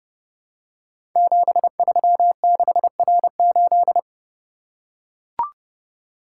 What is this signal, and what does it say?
Morse code, heard as a keyed tone, about 30 wpm, 700 Hz. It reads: 736R8
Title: Code Groups - 5 Letter:Numbers - 30wpm